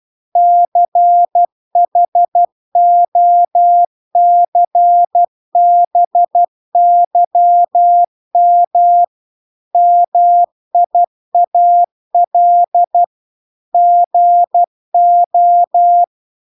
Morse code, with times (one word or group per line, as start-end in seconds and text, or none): 0.4-9.0 CHOCBYM
9.8-13.0 MIAL
13.7-16.0 GO